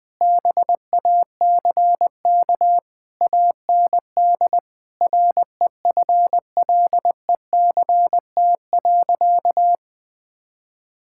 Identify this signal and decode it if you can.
Morse code, heard as a keyed tone, about 19 wpm, 700 Hz. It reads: BACK AND REFLECT.
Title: Sentences with 6-Word Phrases - Spoken Beginning - 20wpm